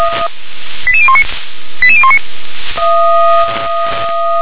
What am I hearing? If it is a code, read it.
Five-tone selective call (ZVEI), heard as quick runs of five tones, tones of about 70 ms. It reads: 88019 88019
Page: SDR/ZVEI